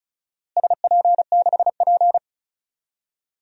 Morse code, heard as a keyed tone, about 36 wpm, 700 Hz. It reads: SP6P
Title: DXCC-100 Call Signs - 4 Characters - Repeated 2x - 35wpm